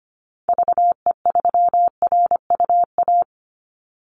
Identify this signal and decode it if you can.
Morse code, heard as a keyed tone, about 25 wpm, 700 Hz. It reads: VE3RUA
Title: Call Signs Repeated 2x - 25wpm